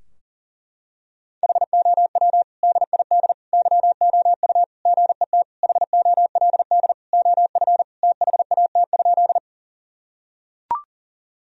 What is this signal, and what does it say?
Morse code, heard as a keyed tone, about 40 wpm, 700 Hz. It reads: HOW DID YOU GET HOLD OF THAT?
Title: Sentences from Top 1000 Words - Review Part 2 - 40wpm